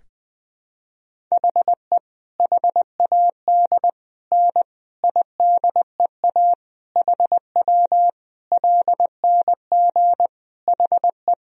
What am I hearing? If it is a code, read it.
Morse code, heard as a keyed tone, about 20 wpm, 700 Hz. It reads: HE HAD N IDEA HW LNG HE
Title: Sentences from Top 300 Words and Abbreviations 20wpm